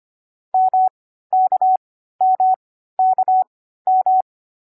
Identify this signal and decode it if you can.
Morse code, heard as a keyed tone, about 25 wpm, spaced as for 14 wpm, 750 Hz. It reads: MKMKM
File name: CW-Koch / koch.2.4.groups5.15wpm